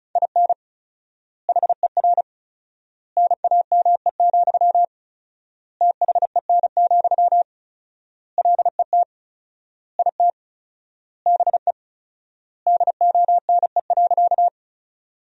Morse code, heard as a keyed tone, about 36 wpm, 700 Hz. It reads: IN HER NAME, THEN, LET IT BE DONE.
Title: 4x Word-Spacing - Sentences from Top 500 Words - 35wpm